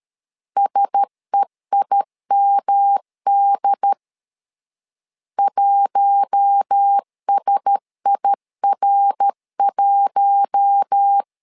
Morse code, heard as a keyed tone, about 13 wpm, 800 Hz. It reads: SEIMD 1SIR1
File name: LEZIONI / 4 Q1S 12wpm 1802